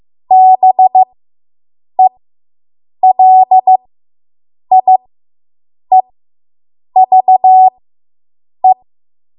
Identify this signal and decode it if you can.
Morse code, heard as a keyed tone, about 15 wpm, spaced as for 7 wpm, 750 Hz. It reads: BELIEVE